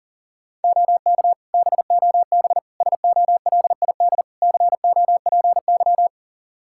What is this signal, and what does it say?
Morse code, heard as a keyed tone, about 37 wpm, 700 Hz. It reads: OK BOB SOLID COPY
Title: Practice QSO - Exchange 3 - More Info - 40wpm